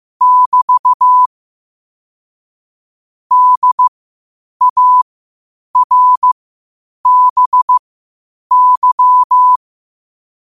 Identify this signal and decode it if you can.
Morse code, heard as a keyed tone, about 15 wpm, spaced as for 8 wpm, 1000 Hz. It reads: <BT> DARBY
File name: qotdfiles / 8wpm